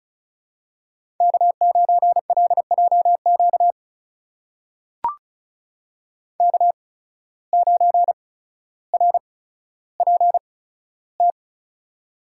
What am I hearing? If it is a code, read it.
Morse code, heard as a keyed tone, about 34 wpm, spaced as for 15 wpm, 700 Hz. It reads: K9LJQ K9RPT
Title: USA Call Signs - 8x Character Spacing - 35wpm